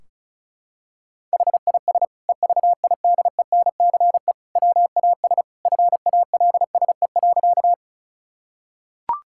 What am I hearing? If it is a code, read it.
Morse code, heard as a keyed tone, about 36 wpm, 700 Hz. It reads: HIS EVIDENCE WAS FALSE.
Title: Sentences from Top 1600 Words 35wpm